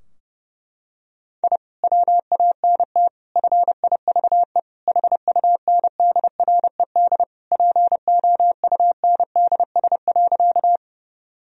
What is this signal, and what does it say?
Morse code, heard as a keyed tone, about 30 wpm, 700 Hz. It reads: I WANT FIVE HUNDRED POUNDS.
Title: Sentences from Top 1000 Words - Review Part 2 - 30wpm